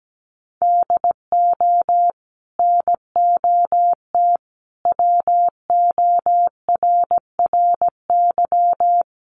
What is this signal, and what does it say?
Morse code, heard as a keyed tone, about 17 wpm, 700 Hz. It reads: DO NOT WORRY
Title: Sets of 5 Words 17wpm